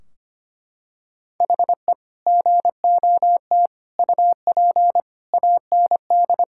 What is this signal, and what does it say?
Morse code, heard as a keyed tone, about 25 wpm, 700 Hz. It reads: HE GOT UP AND
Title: Sentences from Top 1600 Words 25wpm